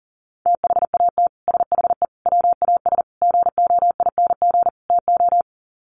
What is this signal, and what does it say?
Morse code, heard as a keyed tone, about 40 wpm, 700 Hz. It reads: THAT SHE WAS GOING TO
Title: Sets of 5 Words Encore2 40wpm